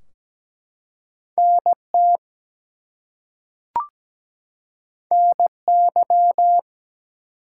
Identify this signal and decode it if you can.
Morse code, heard as a keyed tone, about 17 wpm, 700 Hz. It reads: NT NY
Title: US States and Canadian Provinces 17wpm